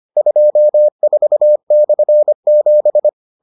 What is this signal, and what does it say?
Morse code, heard as a keyed tone, about 25 wpm, 600 Hz. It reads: 24/7